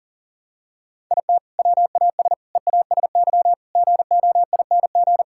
Morse code, heard as a keyed tone, about 40 wpm, 700 Hz. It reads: IT WAS EASY GOING
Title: Sentences from Top 1000 Words - Review Part 5 - 40wpm